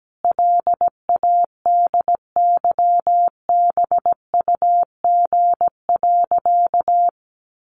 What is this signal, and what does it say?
Morse code, heard as a keyed tone, about 17 wpm, 700 Hz. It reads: LADYBUG.
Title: Additional Easy Phrases 17wpm